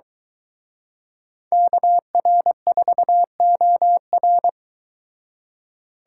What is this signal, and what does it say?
Morse code, heard as a keyed tone, about 23 wpm, 700 Hz. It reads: KR4OR